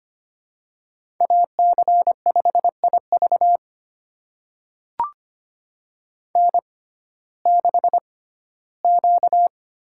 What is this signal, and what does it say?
Morse code, heard as a keyed tone, about 25 wpm, spaced as for 15 wpm, 700 Hz. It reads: AC5IV N6Q
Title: USA Call Signs - 6x Character Spacing - 25wpm